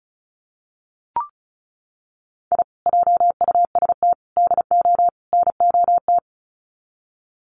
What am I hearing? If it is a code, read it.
Morse code, heard as a keyed tone, about 34 wpm, 700 Hz. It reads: I JUST DO NOT
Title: Sets of 4 Words 35wpm